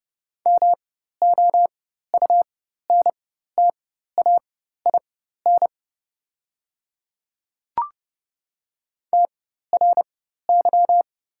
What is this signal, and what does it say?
Morse code, heard as a keyed tone, about 30 wpm, spaced as for 14 wpm, 700 Hz. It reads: MOUNTAIN TRY
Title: 4x Character Spacing - Top 300 Words - 30wpm